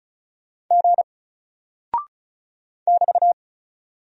Morse code, heard as a keyed tone, about 36 wpm, 700 Hz. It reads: G <BT>
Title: Single Character (Letter:Number:Punctuation:Prosign) Alt 35wpm